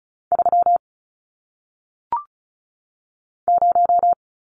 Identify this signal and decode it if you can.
Morse code, heard as a keyed tone, about 35 wpm, 700 Hz. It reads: 3 0
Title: Numbers - 1-Digit 35wpm